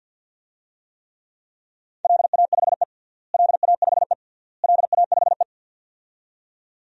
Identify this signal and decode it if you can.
Morse code, heard as a keyed tone, about 52 wpm, 700 Hz. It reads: LA5E LA5E LA5E
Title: DXCC-100 Call Signs - 4 Characters - Repeated 3x - 50wpm